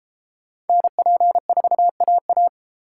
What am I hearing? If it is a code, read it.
Morse code, heard as a keyed tone, about 31 wpm, 700 Hz. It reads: NP4AA